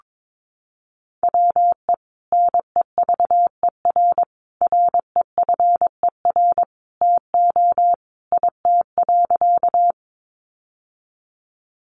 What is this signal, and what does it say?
Morse code, heard as a keyed tone, about 22 wpm, 700 Hz. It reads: WE NEVER REFER TO IT.
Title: Sentences from Top 1000 Words 22wpm